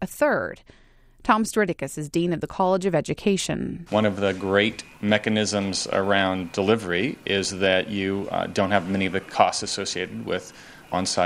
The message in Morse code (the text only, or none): none